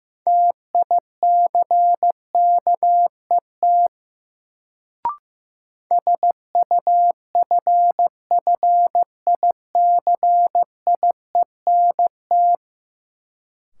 Morse code, heard as a keyed tone, about 15 wpm, 700 Hz. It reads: TICKET SUFFICIENT
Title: Sentences from Top 1400 Words - New Word Drill Alt - 15wpm